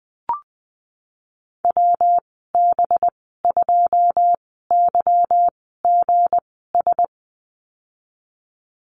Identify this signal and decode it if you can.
Morse code, heard as a keyed tone, about 20 wpm, spaced as for 15 wpm, 700 Hz. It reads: WB2YGS
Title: 193 Lesson 31g - Letter v - Call Signs 20wpm